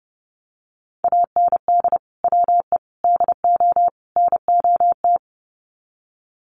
Morse code, heard as a keyed tone, about 30 wpm, 700 Hz. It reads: AND WE DO NOT
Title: Sets of 4 Words 30wpm